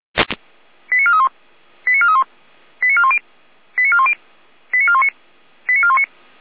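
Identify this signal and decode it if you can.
Five-tone selective call (ZVEI), heard as quick runs of five tones, tones of about 70 ms. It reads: 89431 89431 89410 89410 89419 89419
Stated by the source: ZVEI five-tone call received by SDR.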